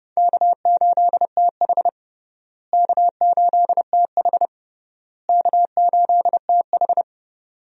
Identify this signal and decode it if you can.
Morse code, heard as a keyed tone, about 30 wpm, 700 Hz. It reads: K8TH K8TH K8TH
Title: USA Call Signs - 4 Characters - Repeated 3x - 30wpm